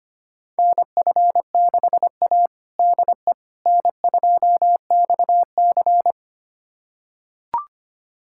Morse code, heard as a keyed tone, about 25 wpm, 700 Hz. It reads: NF6A DE N2XC
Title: USA Callsign DE Callsign - 4 Characters - 25wpm